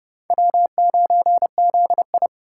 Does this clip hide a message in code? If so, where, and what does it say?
Yes: Morse code from 0.3 to 2.3 s, heard as a keyed tone, about 31 wpm, 700 Hz. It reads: W9ZI